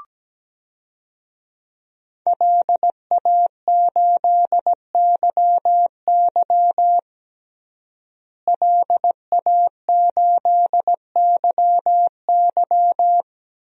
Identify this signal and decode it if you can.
Morse code, heard as a keyed tone, about 17 wpm, 700 Hz. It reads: LA8YY LA8YY
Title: DXCC-100 Call Signs - 5 Characters - Repeated 3x - 17wpm